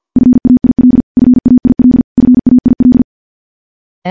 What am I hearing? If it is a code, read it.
Morse code, heard as a keyed tone, about 50 wpm, 250 Hz. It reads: FAIL FAIL FAIL
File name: morse / morse-training-50WPM